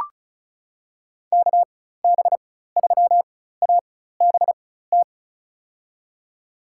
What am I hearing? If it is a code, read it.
Morse code, heard as a keyed tone, about 34 wpm, spaced as for 16 wpm, 700 Hz. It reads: KB3ABT
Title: USA Call Signs - 6 Characters - 4x Character Spacing - 35wpm